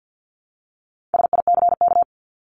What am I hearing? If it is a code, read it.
Morse code, heard as a keyed tone, about 50 wpm, 700 Hz. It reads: SICK
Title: Top 1501-1750 Words 50wpm